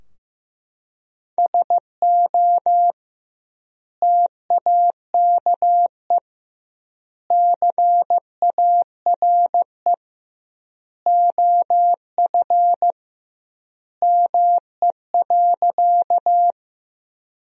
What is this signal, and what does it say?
Morse code, heard as a keyed tone, about 15 wpm, 700 Hz. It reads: SO TAKE CARE OF ME.
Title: 2x Word-Spacing - Sentences from Top 300 Words - 15wpm